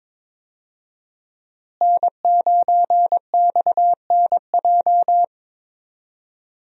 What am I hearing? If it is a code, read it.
Morse code, heard as a keyed tone, about 22 wpm, 700 Hz. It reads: N9XNJ
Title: USA Call Signs - 5 Characters - Repeated 3x - 22wpm